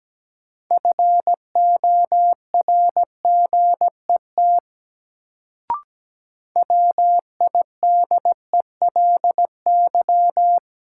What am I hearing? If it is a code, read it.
Morse code, heard as a keyed tone, about 17 wpm, 700 Hz. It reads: FORGET WIDELY
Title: Sentences from Top 1200 Words - New Word Drill Alt - 17wpm